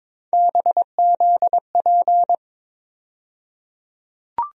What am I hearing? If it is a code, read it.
Morse code, heard as a keyed tone, about 22 wpm, 700 Hz. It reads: BZP
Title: USA Call Sign Postfixes - 3 Letters 22wpm